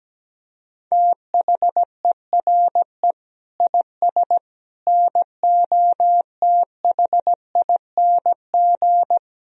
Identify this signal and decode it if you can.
Morse code, heard as a keyed tone, about 17 wpm, 700 Hz. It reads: THERE IS NOTHING
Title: Sentences from Top 500 Words and Abbreviations 17wpm